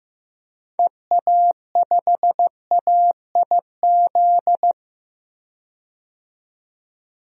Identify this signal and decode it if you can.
Morse code, heard as a keyed tone, about 16 wpm, 700 Hz. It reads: EA5AIZ
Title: DXCC-100 Call Signs - 6 Characters - 15wpm